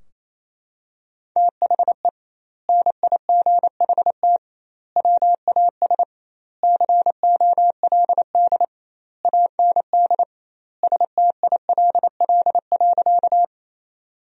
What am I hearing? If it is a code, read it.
Morse code, heard as a keyed tone, about 28 wpm, 700 Hz. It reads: THE NIGHT WAS COLD AND STILL.
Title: 2x Word-Spacing - Sentences from Top 400 Words - 28wpm